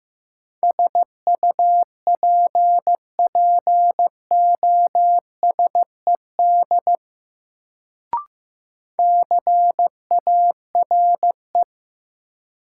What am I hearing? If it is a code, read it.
Morse code, heard as a keyed tone, about 15 wpm, 700 Hz. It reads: SUPPOSED CARE
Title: Sets of 3 Words - New Word Drill - 15wpm